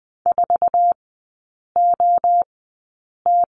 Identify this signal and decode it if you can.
Morse code, heard as a keyed tone, about 20 wpm, spaced as for 8 wpm, 700 Hz. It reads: 4OT
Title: 047 Lesson 9e - Number 4 - Three Characters 20wpm